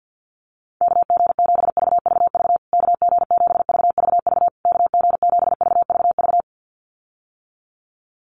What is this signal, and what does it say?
Morse code, heard as a keyed tone, about 49 wpm, 700 Hz. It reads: KG7VVV KG7VVV KG7VVV
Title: USA Call Signs Repeated 3x 50wpm